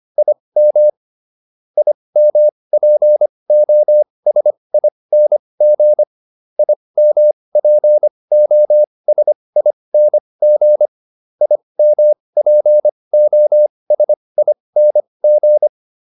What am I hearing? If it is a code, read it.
Morse code, heard as a keyed tone, about 25 wpm, spaced as for 20 wpm, 600 Hz. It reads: IM IMPOSING IMPOSING IMPOSING
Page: /advanced-practice-files/